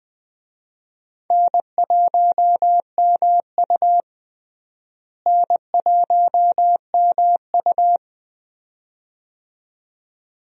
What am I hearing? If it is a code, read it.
Morse code, heard as a keyed tone, about 20 wpm, 700 Hz. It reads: N1MU N1MU
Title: USA Call Signs - 4 Characters - Repeated 2x - 20wpm